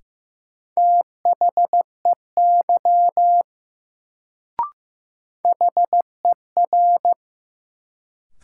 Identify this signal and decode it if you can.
Morse code, heard as a keyed tone, about 15 wpm, 700 Hz. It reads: THEY HER
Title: Sentences from Top 100 Words - New Word Drill - 15wpm